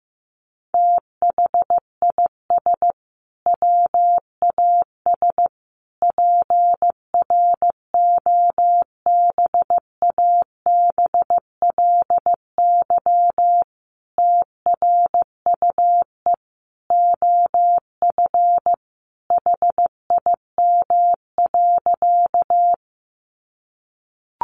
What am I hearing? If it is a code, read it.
Morse code, heard as a keyed tone, about 15 wpm, 700 Hz. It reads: THIS WAS PROBABLY TRUE OF HIM.
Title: Sentences from Top 700 Words 15wpm